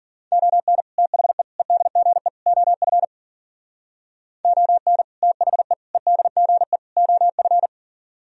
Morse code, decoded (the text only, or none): ON THE EDGE OF ON THE EDGE OF